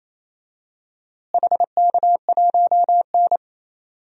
Morse code, heard as a keyed tone, about 28 wpm, 700 Hz. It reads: HK1N
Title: DXCC-100 Call Signs - 4 Characters - Repeated 3x - 28wpm